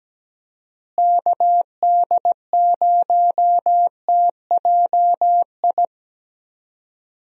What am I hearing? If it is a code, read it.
Morse code, heard as a keyed tone, about 17 wpm, 700 Hz. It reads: KD0TJI
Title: USA Call Signs - 6 Characters - Repeated 3x - 17wpm